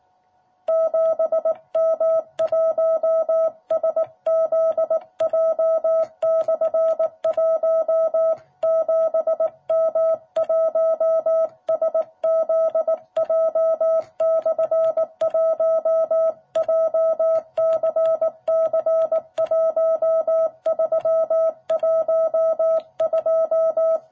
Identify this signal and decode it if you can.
Morse code, heard as a keyed tone, about 18 wpm, 660 Hz. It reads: 7M1SZJ/1 7M1SZJ/1 JCC1312
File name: jh8cbh / 2021cw